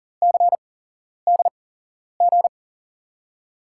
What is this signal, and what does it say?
Morse code, heard as a keyed tone, about 42 wpm, spaced as for 11 wpm, 700 Hz. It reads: CDG
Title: USA Call Signs - 5 Characters - 8x Character Spacing - 40wpm